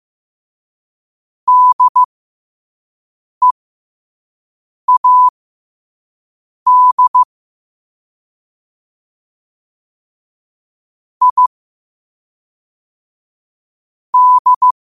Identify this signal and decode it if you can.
Morse code, heard as a keyed tone, about 15 wpm, spaced as for 5 wpm, 1000 Hz. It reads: DEAD I D
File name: 5wpm